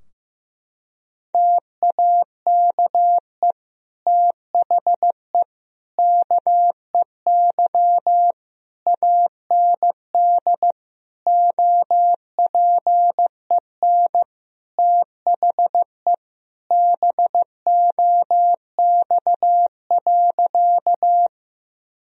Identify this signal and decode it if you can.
Morse code, heard as a keyed tone, about 15 wpm, 700 Hz. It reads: TAKE THE KEY AND OPEN THE BOX.
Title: Sentences from Top 1000 Words - Review Part 4 - 15wpm